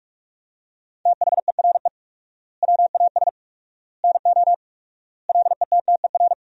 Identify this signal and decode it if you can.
Morse code, heard as a keyed tone, about 47 wpm, 700 Hz. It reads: THERE WAS NO LETTER
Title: 4x Word-Spacing - Sentences from Top 300 Words - 45wpm